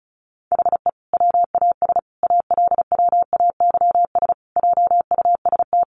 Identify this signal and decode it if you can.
Morse code, heard as a keyed tone, about 34 wpm, 700 Hz. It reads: HE WAS ALWAYS JUST
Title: Sentences from Top 600 Words 35wpm